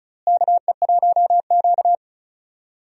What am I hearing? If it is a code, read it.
Morse code, heard as a keyed tone, about 36 wpm, 700 Hz. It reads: KE1Q
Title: USA Call Signs - 4 Characters - Repeated 2x - 35wpm